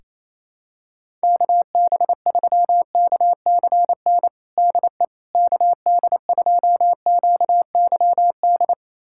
Morse code, heard as a keyed tone, about 28 wpm, 700 Hz. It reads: KB3KCN DE KD2QYD
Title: USA Callsign DE Callsign - 6 Characters - 28wpm